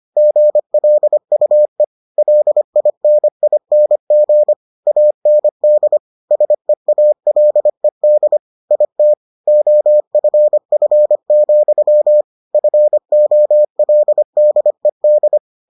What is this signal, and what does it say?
Morse code, heard as a keyed tone, about 25 wpm, 600 Hz. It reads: GLUE LINING AND SEALED IT OFF, FOLDED